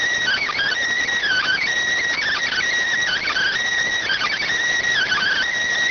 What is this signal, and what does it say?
Emergency Alert System (SAME) tones (header bursts), heard as harsh buzzing bursts. It is heard from 1.2 to 5.5 s.